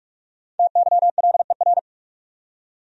Morse code, heard as a keyed tone, about 46 wpm, 700 Hz. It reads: TYLER
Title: USA Common Names - Max 5 Letters - 45wpm